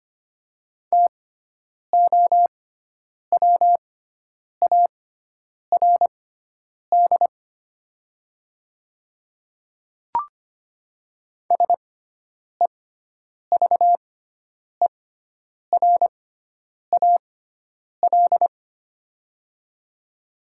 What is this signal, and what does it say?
Morse code, heard as a keyed tone, about 25 wpm, spaced as for 9 wpm, 700 Hz. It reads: TOWARD SEVERAL
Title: 6x Character Spacing - Top 500 Words - 25wpm